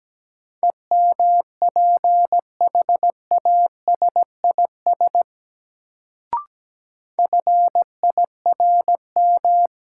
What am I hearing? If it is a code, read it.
Morse code, heard as a keyed tone, about 17 wpm, 700 Hz. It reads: EMPHASIS FIRM
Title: Sentences from Top 1400 Words - New Word Drill - 17wpm